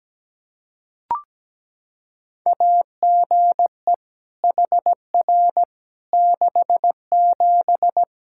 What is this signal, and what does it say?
Morse code, heard as a keyed tone, about 17 wpm, 700 Hz. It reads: AGE HR 67
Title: K1USN SST QSO Practice 17wpm